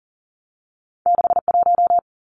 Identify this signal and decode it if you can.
Morse code, heard as a keyed tone, about 40 wpm, 700 Hz. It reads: 6 1
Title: Two Challenging Letter-Number Combinations 40wpm